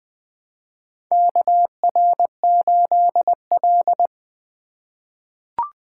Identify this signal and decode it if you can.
Morse code, heard as a keyed tone, about 19 wpm, 700 Hz. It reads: KR8L
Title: K1USN SST QSO Practice - 20wpm